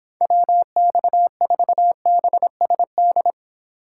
Morse code, heard as a keyed tone, about 26 wpm, 700 Hz. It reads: WX4BSD